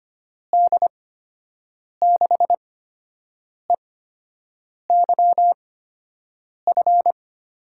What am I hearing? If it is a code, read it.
Morse code, heard as a keyed tone, about 25 wpm, spaced as for 7 wpm, 700 Hz. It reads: D6EYF
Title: USA Call Signs - 6 Characters - 8x Character Spacing - 25wpm